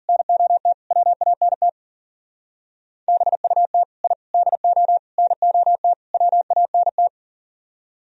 Morse code, heard as a keyed tone, about 50 wpm, 700 Hz. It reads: NOT WANT BUT I DO NOT WANT
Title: Speed-Racing - Sets of 5 Words 40wpm